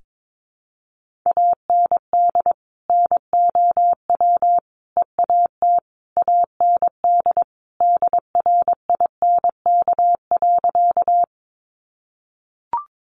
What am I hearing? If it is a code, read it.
Morse code, heard as a keyed tone, about 22 wpm, 700 Hz. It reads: AND NOW EAT AND DRINK.
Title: Sentences from Top 1500 Words 22wpm